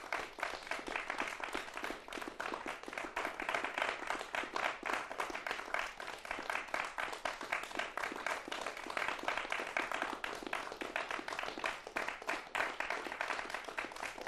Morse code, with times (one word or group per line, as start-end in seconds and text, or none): none